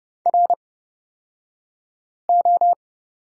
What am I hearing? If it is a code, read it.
Morse code, heard as a keyed tone, about 29 wpm, spaced as for 5 wpm, 700 Hz. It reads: RO
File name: Lesson 36d - BT - Two Characters 30wpm